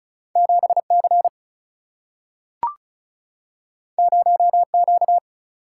Morse code, heard as a keyed tone, about 36 wpm, 700 Hz. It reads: 7 C 0 Q
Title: Single-Letter Single-Number 35wpm